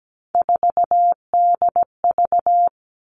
Morse code, heard as a keyed tone, about 17 wpm, 700 Hz. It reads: 4DV
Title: Three Challenging Letter-Number Combinations 17wpm